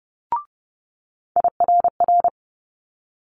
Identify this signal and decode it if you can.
Morse code, heard as a keyed tone, about 30 wpm, 700 Hz. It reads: IRR
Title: Multi-letter Phonemes Quick Fire 30wpm